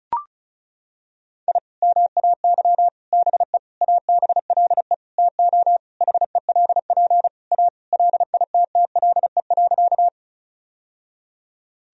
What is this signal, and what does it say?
Morse code, heard as a keyed tone, about 36 wpm, 700 Hz. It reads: I MAY BE ABLE TO HELP A LITTLE.
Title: Sentences from Top 1000 Words - Review Part 2 - 35wpm